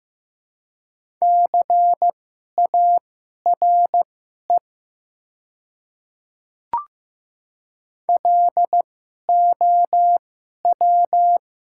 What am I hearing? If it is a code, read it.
Morse code, heard as a keyed tone, about 15 wpm, spaced as for 11 wpm, 690 Hz. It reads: CARE LOW
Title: 2x Character Spacing - Top 300 Words - 15wpm